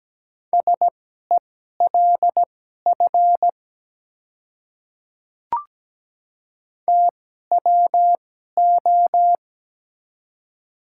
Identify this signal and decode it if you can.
Morse code, heard as a keyed tone, about 17 wpm, spaced as for 12 wpm, 700 Hz. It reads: SELF TWO
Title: 2x Character Spacing - Top 300 Words - 17wpm